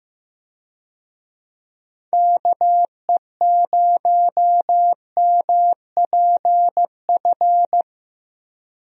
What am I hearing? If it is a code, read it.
Morse code, heard as a keyed tone, about 15 wpm, 700 Hz. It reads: KE0MPF
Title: USA Call Signs - 6 Characters - 15wpm